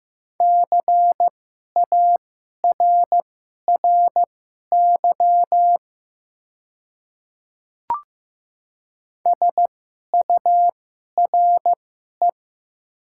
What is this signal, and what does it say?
Morse code, heard as a keyed tone, about 15 wpm, spaced as for 11 wpm, 700 Hz. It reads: CARRY SURE
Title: 2x Character Spacing - Top 300 Words - 15wpm